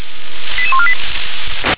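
Five-tone selective call (ZVEI), heard as quick runs of five tones, tones of about 70 ms. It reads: 88148
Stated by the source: ZVEI five-tone selective call 88154 received by SDR.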